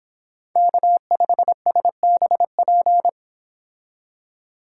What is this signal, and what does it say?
Morse code, heard as a keyed tone, about 28 wpm, 700 Hz. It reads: K5SBP